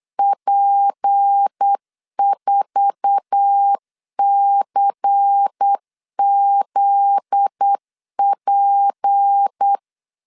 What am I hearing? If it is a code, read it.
Morse code, heard as a keyed tone, about 8 wpm, 800 Hz. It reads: P4CZP
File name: LEZIONI / 8 YJ45 / 8 YJ45 8wpm 3214